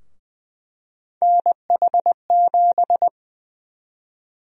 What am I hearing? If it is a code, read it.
Morse code, heard as a keyed tone, about 20 wpm, 700 Hz. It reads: NH7
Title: USA Call Sign Prefixes - Repeated 2x - 20wpm